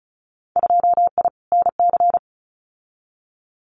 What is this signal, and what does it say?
Morse code, heard as a keyed tone, about 34 wpm, 700 Hz. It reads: 2I NC
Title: Winter Field Day Exchanges 35wpm